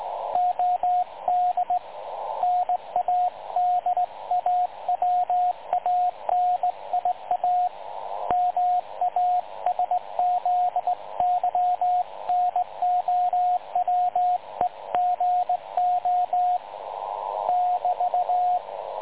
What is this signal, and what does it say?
Morse code, heard as a keyed tone, about 19 wpm, 720 Hz, with radio noise. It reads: OD NADAWANIA MASZYNOWEGO <BT>
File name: dlazaawansowanych / nagraniazpasma / websdr recording 3565.0kHz